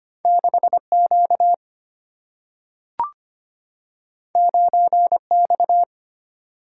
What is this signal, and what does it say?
Morse code, heard as a keyed tone, about 25 wpm, 700 Hz. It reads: 6 Q 9 X
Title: Single-Letter Single-Number 25wpm